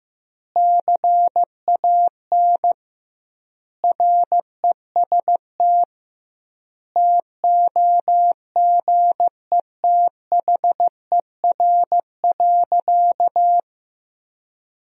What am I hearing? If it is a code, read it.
Morse code, heard as a keyed tone, about 16 wpm, 700 Hz. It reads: CAN REST TOGETHER.
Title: 2x Word-Spacing - Sentences from Top 500 Words - 15wpm